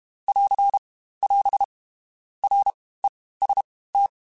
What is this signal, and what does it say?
Morse code, heard as a keyed tone, about 31 wpm, spaced as for 18 wpm, 800 Hz. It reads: <AR> <AS> REST
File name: Basic Letter Groups By7Groups / RKPX / RKPX@30